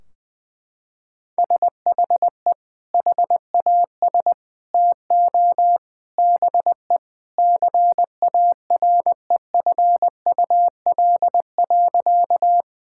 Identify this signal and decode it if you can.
Morse code, heard as a keyed tone, about 20 wpm, 700 Hz. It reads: SHE HAS TO BE CAREFUL.
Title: Sentences from Top 1600 Words 20wpm